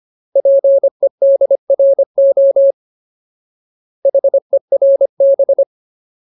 Morse code, heard as a keyed tone, about 25 wpm, 550 Hz. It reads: PEDRO HERB
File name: Ultimate Word List / 25WPM Ultimate